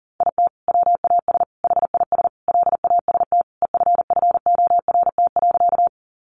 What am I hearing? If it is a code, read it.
Morse code, heard as a keyed tone, about 40 wpm, 700 Hz. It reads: IT WAS HIS LAST EFFORT.
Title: Sentences from Top 1200 Words 40wpm